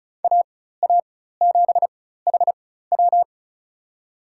Morse code, heard as a keyed tone, about 36 wpm, spaced as for 16 wpm, 700 Hz. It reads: AA7HW
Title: USA Call Signs - 5 Characters - 4x Character Spacing - 35wpm